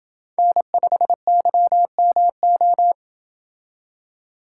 Morse code, heard as a keyed tone, about 28 wpm, 700 Hz. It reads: N5YMO